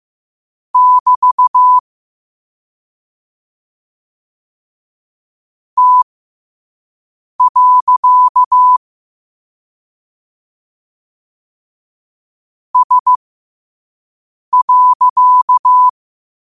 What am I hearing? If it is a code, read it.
Morse code, heard as a keyed tone, about 15 wpm, spaced as for 5 wpm, 1000 Hz. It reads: <BT> T. S.
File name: qotdfiles / 5wpm